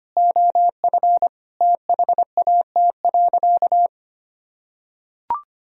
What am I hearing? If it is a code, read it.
Morse code, heard as a keyed tone, about 25 wpm, 700 Hz. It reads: OF THAT.
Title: Sentences from Top 1000 Words - Review Part 3 - 25wpm